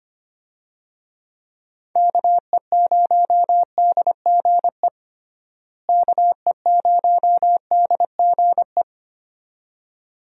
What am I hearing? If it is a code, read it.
Morse code, heard as a keyed tone, about 25 wpm, 700 Hz. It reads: KE0DGE KE0DGE
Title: USA Call Signs - 6 Characters - Repeated 2x - 25wpm